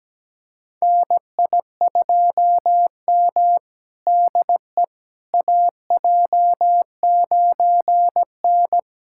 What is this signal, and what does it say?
Morse code, heard as a keyed tone, about 17 wpm, 700 Hz. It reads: NI2M DE AJ9N
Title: USA Callsign DE Callsign - 4 Characters - 17wpm